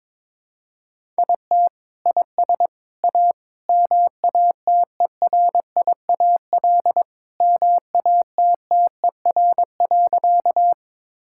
Morse code, heard as a keyed tone, about 22 wpm, 700 Hz. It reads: IT IS A MATERIAL MATTER.
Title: Sentences from Top 1000 Words - Review Part 5 - 22wpm